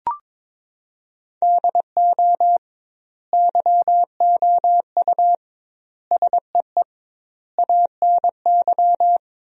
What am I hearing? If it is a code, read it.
Morse code, heard as a keyed tone, about 21 wpm, 700 Hz. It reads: DO YOU SEE ANY
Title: 2x Word-Spacing - Sentences from Top 500 Words - 22wpm